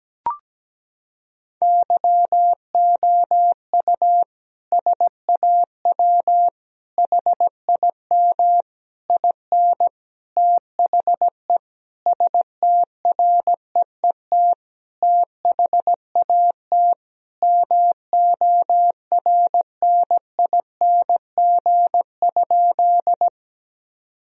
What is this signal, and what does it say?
Morse code, heard as a keyed tone, about 17 wpm, 700 Hz. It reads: YOU SAW HIM IN THE STREET THAT MORNING?
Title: Sentences from Top 1000 Words - Review Part 3 - 17wpm